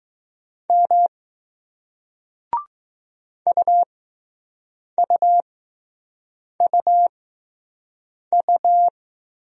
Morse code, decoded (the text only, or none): M UUUU